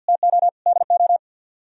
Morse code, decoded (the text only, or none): TO DO